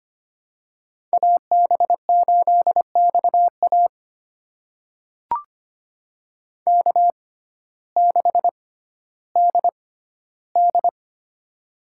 Morse code, heard as a keyed tone, about 25 wpm, spaced as for 14 wpm, 700 Hz. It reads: AB8XA K6DD
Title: USA Call Signs - 6x Character Spacing - 25wpm